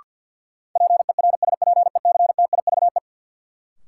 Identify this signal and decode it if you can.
Morse code, heard as a keyed tone, about 51 wpm, 700 Hz. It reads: PERSPECTIVE
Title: Sentences from Top 1200 Words - New Word Drill - 50wpm